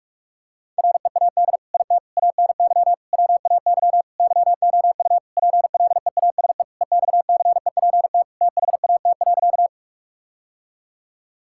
Morse code, heard as a keyed tone, about 46 wpm, 700 Hz. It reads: READ IT ANY WAY YOU PLEASE EXCEPT THAT.
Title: Sentences from Top 1600 Words 45wpm